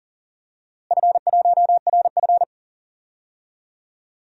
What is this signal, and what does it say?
Morse code, heard as a keyed tone, about 40 wpm, 700 Hz. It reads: F1RF
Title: DXCC-100 Call Signs - 4 Characters - Repeated 2x - 40wpm